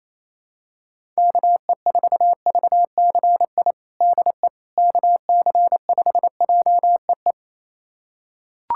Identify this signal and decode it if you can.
Morse code, heard as a keyed tone, about 27 wpm, 700 Hz. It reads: KE4VCI DE KC5JEE
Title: USA Callsign DE Callsign - 6 Characters - 28wpm